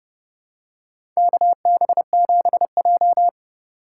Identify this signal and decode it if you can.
Morse code, heard as a keyed tone, about 30 wpm, 700 Hz. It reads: KB7J
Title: USA Call Signs - 4 Characters - 2x Character Spacing - 30wpm